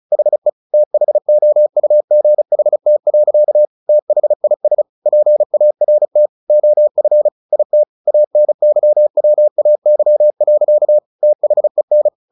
Morse code, heard as a keyed tone, about 34 wpm, 600 Hz. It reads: HE THOUGHT. THIS PART OF IT ANYWAY. THEN
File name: the old man and the sea 2 35wpm